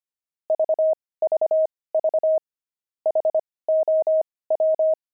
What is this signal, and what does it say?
Morse code, heard as a keyed tone, about 25 wpm, spaced as for 18 wpm, 630 Hz. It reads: VVV HOW
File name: pangram-fs-18-cs-25-frogs